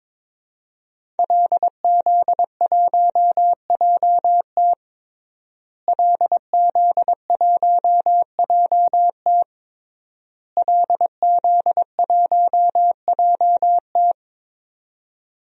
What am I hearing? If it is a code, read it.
Morse code, heard as a keyed tone, about 22 wpm, 700 Hz. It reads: LZ1JT LZ1JT LZ1JT
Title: DXCC-100 Call Signs - 5 Characters - Repeated 3x - 22wpm